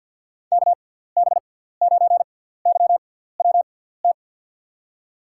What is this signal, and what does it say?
Morse code, heard as a keyed tone, about 48 wpm, spaced as for 17 wpm, 700 Hz. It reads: KB9YWT
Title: USA Call Signs - 6 Characters - 6x Character Spacing - 50wpm